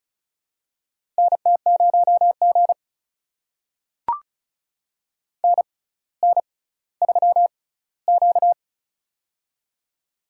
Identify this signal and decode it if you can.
Morse code, heard as a keyed tone, about 36 wpm, spaced as for 18 wpm, 700 Hz. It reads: NT0G NN3Q
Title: USA Call Signs - 4 Characters - 6x Character Spacing - 35wpm